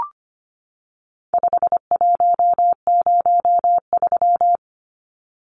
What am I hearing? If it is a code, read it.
Morse code, heard as a keyed tone, about 25 wpm, 700 Hz. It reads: 5103
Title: Numbers - 4-Digit 25wpm